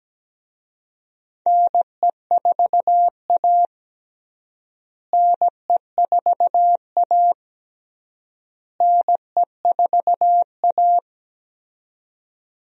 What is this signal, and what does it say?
Morse code, heard as a keyed tone, about 17 wpm, 700 Hz. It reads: NE4A NE4A NE4A
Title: USA Call Signs - 4 Characters - Repeated 3x - 17wpm